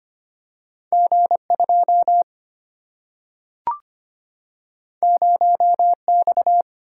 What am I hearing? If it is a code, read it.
Morse code, heard as a keyed tone, about 26 wpm, 700 Hz. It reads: G2 0X
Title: Single-Letter Single-Number 25wpm